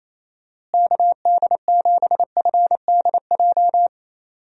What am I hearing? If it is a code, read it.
Morse code, heard as a keyed tone, about 28 wpm, 700 Hz. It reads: KD7FDJ
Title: USA Call Signs - 6 Characters - Repeated 3x - 28wpm